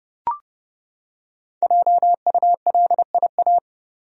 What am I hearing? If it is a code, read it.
Morse code, heard as a keyed tone, about 30 wpm, 700 Hz. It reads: JULIA
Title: USA Common Names - Max 5 Letters - 30wpm